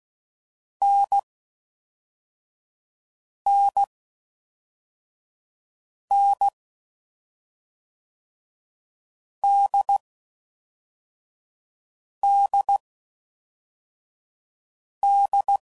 Morse code, heard as a keyed tone, about 16 wpm, spaced as for 4 wpm, 800 Hz. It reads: NNN DDD
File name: Mixed / Short Runs / NDB NDB6